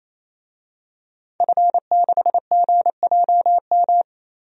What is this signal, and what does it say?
Morse code, heard as a keyed tone, about 28 wpm, 700 Hz. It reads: F6GJM
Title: DXCC-100 Call Signs - 5 Characters - Repeated 3x - 28wpm